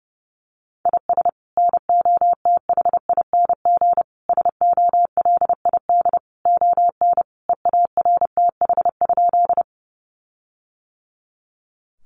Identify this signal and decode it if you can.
Morse code, heard as a keyed tone, about 30 wpm, 700 Hz. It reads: IS NOTHING SOLID ON EARTH?
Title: Sentences from Top 1500 Words 30wpm